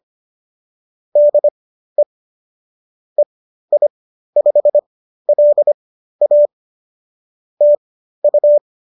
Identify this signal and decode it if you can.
Morse code, heard as a keyed tone, about 25 wpm, spaced as for 13 wpm, 590 Hz. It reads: DE EI5LA TU